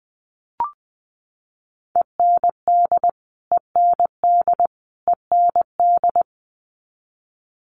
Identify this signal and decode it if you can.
Morse code, heard as a keyed tone, about 20 wpm, 700 Hz. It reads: END END END
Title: Top 200 Words Repeated 20wpm